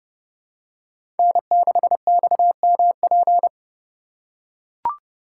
Morse code, heard as a keyed tone, about 32 wpm, 700 Hz. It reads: N6XMP
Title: USA Call Signs - 5 Characters - 2x Character Spacing - 30wpm